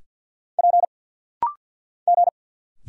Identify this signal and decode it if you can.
Morse code, heard as a keyed tone, about 50 wpm, 700 Hz. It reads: P G
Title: Single Letters - Rapid-Fire 50wpm